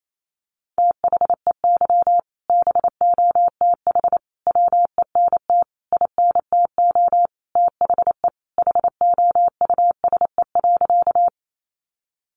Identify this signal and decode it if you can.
Morse code, heard as a keyed tone, about 28 wpm, 700 Hz. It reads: THEY BOTH WENT INTO THE HOUSE.
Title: Sentences from Top 600 Words 28wpm